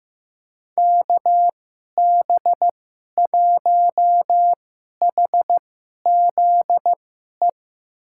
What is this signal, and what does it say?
Morse code, heard as a keyed tone, about 15 wpm, spaced as for 11 wpm, 700 Hz. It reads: KB1HZE
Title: USA Call Signs - 6 Characters - 2x Character Spacing - 15wpm